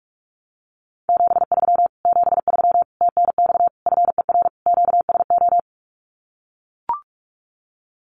Morse code, heard as a keyed tone, about 44 wpm, 700 Hz. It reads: 73 73 TNX FER QSO
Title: Common QSO Phrases Encore 45wpm